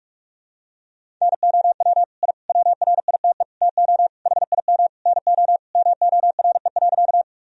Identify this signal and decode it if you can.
Morse code, heard as a keyed tone, about 46 wpm, 700 Hz. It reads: NOW I WRITE TO HIM NO MORE.
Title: Sentences from Top 1000 Words - Review Part 1 - 45wpm